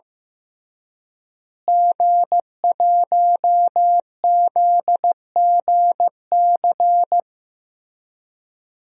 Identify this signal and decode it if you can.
Morse code, heard as a keyed tone, about 15 wpm, 700 Hz. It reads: G1ZGC
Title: DXCC-100 Call Signs - 5 Characters - Repeated 3x - 15wpm